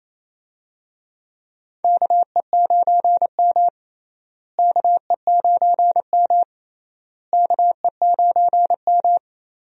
Morse code, heard as a keyed tone, about 28 wpm, 700 Hz. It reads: KE9M KE9M KE9M
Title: USA Call Signs - 4 Characters - Repeated 3x - 28wpm